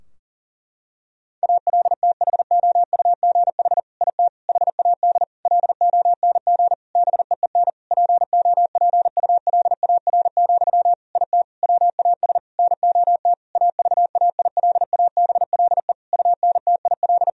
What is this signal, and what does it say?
Morse code, heard as a keyed tone, about 40 wpm, 700 Hz. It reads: ALTHOUGH IT HAD LONG BEEN POPULAR, IT WAS NOT AVAILABLE UNTIL
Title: Sentences from Top 1000 Words - Review Part 3 - 40wpm